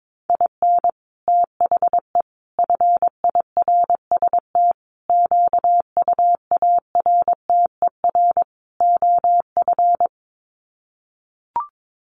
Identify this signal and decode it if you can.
Morse code, heard as a keyed tone, about 22 wpm, 700 Hz. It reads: IN THE FIRST QUARTER OF
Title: Sets of 5 Words Encore2 22wpm